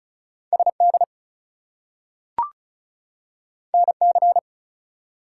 Morse code, heard as a keyed tone, about 36 wpm, 700 Hz. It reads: SD NC
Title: US States and Canadian Provinces 35wpm